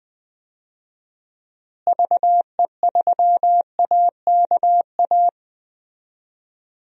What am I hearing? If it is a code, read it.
Morse code, heard as a keyed tone, about 21 wpm, 700 Hz. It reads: VE3AKA
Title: DXCC-100 Call Signs - 6 Characters - Repeated 3x - 20wpm